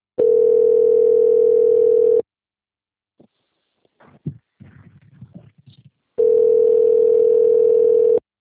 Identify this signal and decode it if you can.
Ringback tone (North America), 440 + 480 Hz.